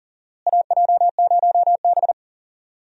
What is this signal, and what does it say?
Morse code, heard as a keyed tone, about 40 wpm, 700 Hz. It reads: AJ0B
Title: USA Call Signs - 4 Characters - Repeated 3x - 40wpm